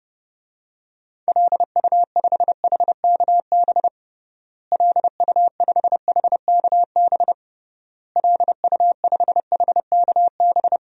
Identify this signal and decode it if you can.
Morse code, heard as a keyed tone, about 32 wpm, 700 Hz. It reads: LU5HKB LU5HKB LU5HKB
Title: DXCC-100 Call Signs - 6 Characters - Repeated 3x - 30wpm